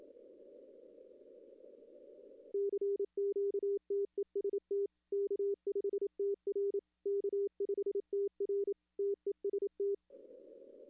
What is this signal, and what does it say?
Morse code, heard as a keyed tone, about 28 wpm, 400 Hz. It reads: CQTEST K5TR K5TR TEST
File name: contests / arrl-160-k5tr-op-w5tn -2300